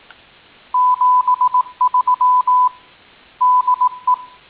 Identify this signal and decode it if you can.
Morse code, heard as a keyed tone, about 18 wpm, 1000 Hz. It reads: 73 DE